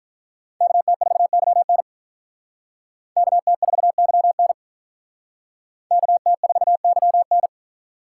Morse code, heard as a keyed tone, about 44 wpm, 700 Hz. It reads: KT4YN KT4YN KT4YN